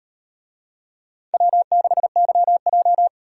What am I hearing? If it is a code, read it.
Morse code, heard as a keyed tone, about 38 wpm, 700 Hz. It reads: W6YJ